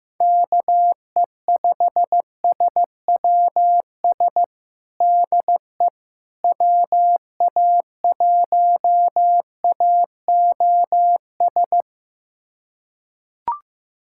Morse code, heard as a keyed tone, about 16 wpm, 700 Hz. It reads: KE5SWS DE WA1AOS
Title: USA Callsign DE Callsign - 6 Characters - 15wpm